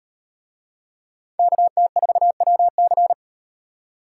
Morse code, heard as a keyed tone, about 35 wpm, 700 Hz. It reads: KT4WC